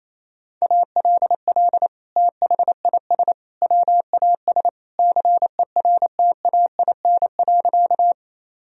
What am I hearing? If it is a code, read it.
Morse code, heard as a keyed tone, about 28 wpm, 700 Hz. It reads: ALL THIS WAS CERTAIN.
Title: Sentences from Top 1000 Words - Review Part 2 - 28wpm